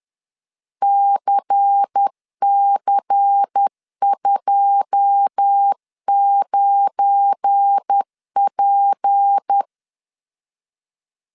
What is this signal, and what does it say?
Morse code, heard as a keyed tone, about 10 wpm, 800 Hz. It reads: CC29P
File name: LEZIONI / 10 SIGNS / 10C DOT 10wpm 2940